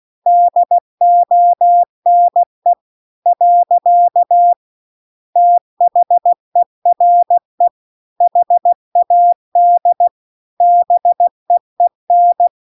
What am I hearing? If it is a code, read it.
Morse code, heard as a keyed tone, about 16 wpm, 700 Hz. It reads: DONE . THERE HAD BEEN